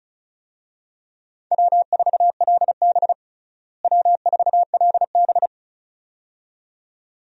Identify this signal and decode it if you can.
Morse code, heard as a keyed tone, about 36 wpm, 700 Hz. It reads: W4LB W4LB
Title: USA Call Signs - 4 Characters - Repeated 2x - 35wpm